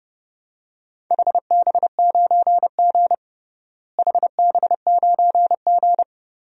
Morse code, heard as a keyed tone, about 30 wpm, 700 Hz. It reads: HB9G HB9G
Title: DXCC-100 Call Signs - 4 Characters - Repeated 3x - 30wpm